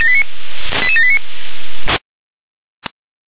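Five-tone selective call (ZVEI), heard as quick runs of five tones, tones of about 70 ms. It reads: 88789
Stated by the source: ZVEI five-tone selective call received on SDR.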